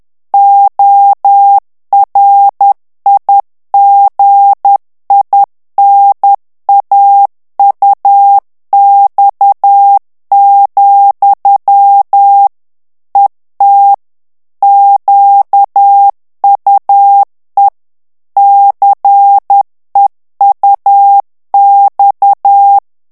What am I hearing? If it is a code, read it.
Morse code, heard as a keyed tone, about 10 wpm, 800 Hz. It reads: ORIGINAUX, ET QUE CEUX
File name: morse mp3 / FRANCAIS TEXTE CLAIR 2 VITESSE 10 MOTS MINUTES